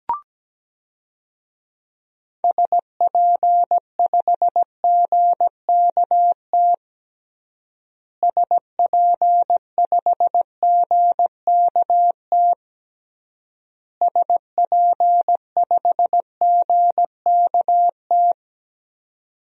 Morse code, heard as a keyed tone, about 17 wpm, 700 Hz. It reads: SP5GKT SP5GKT SP5GKT
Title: DXCC-100 Call Signs - 6 Characters - Repeated 3x - 17wpm